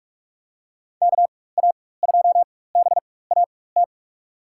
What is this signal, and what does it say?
Morse code, heard as a keyed tone, about 46 wpm, spaced as for 21 wpm, 700 Hz. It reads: KA2BAT
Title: USA Call Signs - 6 Characters - 4x Character Spacing - 45wpm